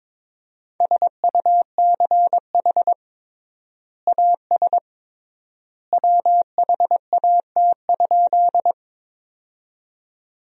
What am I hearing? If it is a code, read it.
Morse code, heard as a keyed tone, about 21 wpm, 700 Hz. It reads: SUCH AS WHAT?
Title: 3x Word-Spacing - Sentences from Top 200 Words - 22wpm